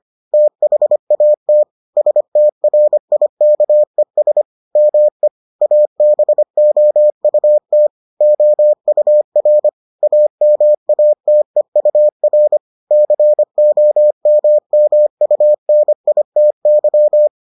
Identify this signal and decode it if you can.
Morse code, heard as a keyed tone, about 25 wpm, 600 Hz. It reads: THAT STRIKES ME ABOUT OUR AMATEUR COMMUNITY